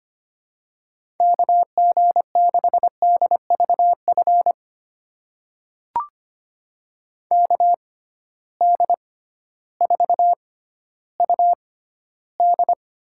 Morse code, heard as a keyed tone, about 25 wpm, spaced as for 13 wpm, 700 Hz. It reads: KG6DVF KD4UD
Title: USA Call Signs - 6 Characters - 6x Character Spacing - 25wpm